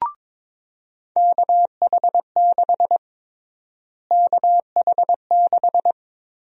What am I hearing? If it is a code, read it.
Morse code, heard as a keyed tone, about 22 wpm, 700 Hz. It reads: KH6 KH6
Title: USA Call Sign Prefixes - Repeated 2x - 22wpm